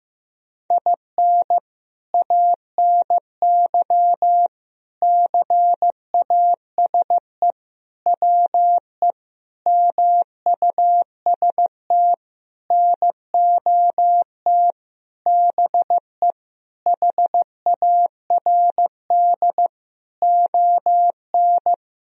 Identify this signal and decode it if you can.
Morse code, heard as a keyed tone, about 15 wpm, 700 Hz. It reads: IN ANY CASE WE MUST NOT BE HARD ON
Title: Sentences from Top 1000 Words - Review Part 1 - 15wpm